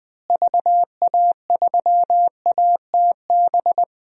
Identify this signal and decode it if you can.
Morse code, heard as a keyed tone, about 20 wpm, 700 Hz. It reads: VA3ATB
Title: DXCC-100 Call Signs - 6 Characters - Repeated 3x - 20wpm